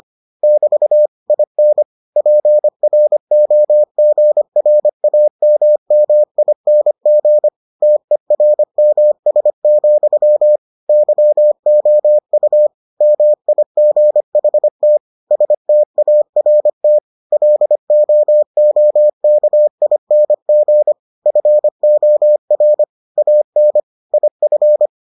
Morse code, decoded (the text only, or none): <BT> IN PROGRAMMING TERMS, YOU MIGHT START LOOKING FOR AN IF